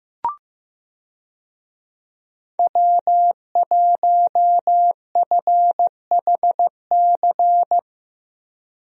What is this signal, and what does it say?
Morse code, heard as a keyed tone, about 15 wpm, 700 Hz. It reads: W1FHC
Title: USA Call Signs - 5 Characters - Repeated 3x - 15wpm